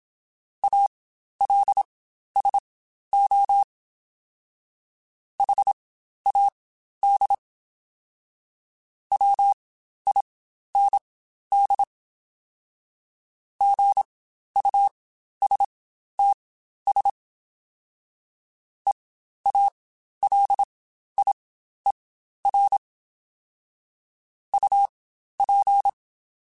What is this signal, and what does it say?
Morse code, heard as a keyed tone, about 26 wpm, spaced as for 12 wpm, 800 Hz. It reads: ALSO HAD WIND GUSTS EALIER UP
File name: Spaced Characters / Spaced Characters-mp3s / Weather5 5 sp @25